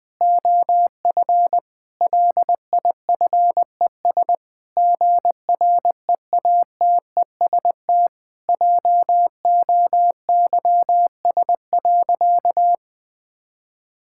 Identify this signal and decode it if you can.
Morse code, heard as a keyed tone, about 21 wpm, 700 Hz. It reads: OF LIFES GREATEST JOYS.
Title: Difficult Sentences with Hints - 20wpm